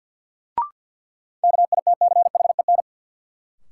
Morse code, heard as a keyed tone, about 52 wpm, 700 Hz. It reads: KITCHEN
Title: Sentences from Top 1400 Words - New Word Drill - 50wpm